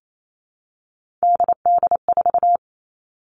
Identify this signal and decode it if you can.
Morse code, heard as a keyed tone, about 28 wpm, 700 Hz. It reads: DD4
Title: All Cut Numbers - 3-Digit 28wpm